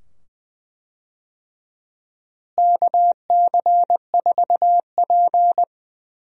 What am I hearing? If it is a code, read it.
Morse code, heard as a keyed tone, about 20 wpm, 700 Hz. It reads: KC4P
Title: USA Call Signs - 4 Characters - Repeated 2x - 20wpm